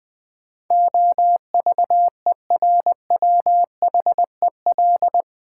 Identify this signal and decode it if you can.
Morse code, heard as a keyed tone, about 20 wpm, 700 Hz. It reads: OVERWHEL
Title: Build A Word - 4 Letters At A Time - 12-Letter Words - 20wpm